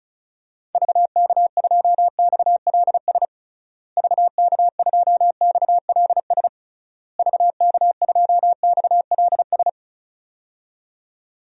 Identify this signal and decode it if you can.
Morse code, heard as a keyed tone, about 35 wpm, 700 Hz. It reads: VK2XLS VK2XLS VK2XLS
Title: DXCC-100 Call Signs - 6 Characters - Repeated 3x - 35wpm